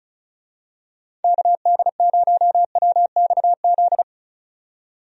Morse code, heard as a keyed tone, about 36 wpm, 700 Hz. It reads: KD0WXZ
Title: USA Call Signs - 6 Characters - 35wpm